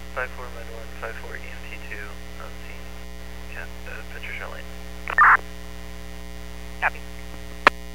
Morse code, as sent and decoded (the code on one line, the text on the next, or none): none